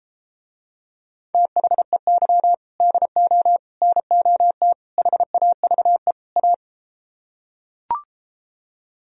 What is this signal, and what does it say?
Morse code, heard as a keyed tone, about 33 wpm, 700 Hz. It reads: THEY DO NOT HAVE A